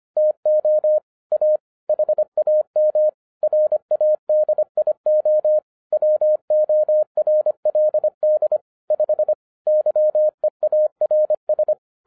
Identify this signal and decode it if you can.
Morse code, heard as a keyed tone, about 25 wpm, 610 Hz. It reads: TO A HAM RADIO WORLD 5 YEARS